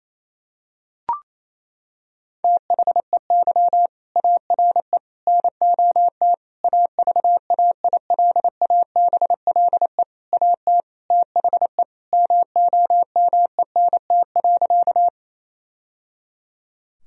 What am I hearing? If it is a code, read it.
Morse code, heard as a keyed tone, about 28 wpm, 700 Hz. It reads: THEY ARE NOT AVAILABLE AT THE MOMENT.
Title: Sentences from Top 1000 Words - Review Part 3 - 28wpm